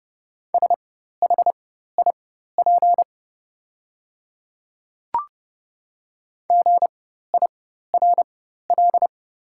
Morse code, heard as a keyed tone, about 30 wpm, spaced as for 14 wpm, 690 Hz. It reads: SHIP GIRL
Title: 4x Character Spacing - Top 500 Words - 30wpm